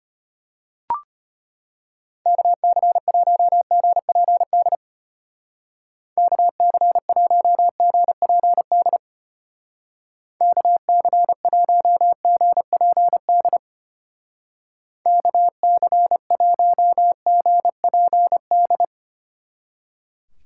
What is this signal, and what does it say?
Morse code, heard as a keyed tone, about 30 wpm, 700 Hz. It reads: KC1GPD KC1GPD KC1GPD KC1GPD
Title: Speed Racing - USA 2x3 Call Signs 25wpm